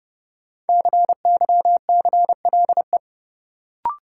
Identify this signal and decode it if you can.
Morse code, heard as a keyed tone, about 30 wpm, 700 Hz. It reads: CYCLE
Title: Sentences from Top 900 Words - New Word Drill - 30wpm